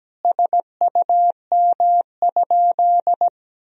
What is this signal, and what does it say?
Morse code, heard as a keyed tone, about 17 wpm, 700 Hz. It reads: SUM?
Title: Sentences from Top 100 Words and Abbreviations 17wpm